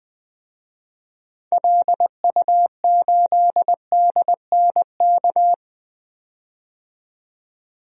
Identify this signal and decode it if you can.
Morse code, heard as a keyed tone, about 21 wpm, 700 Hz. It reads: LU8DNK
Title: DXCC-100 Call Signs - 6 Characters - 20wpm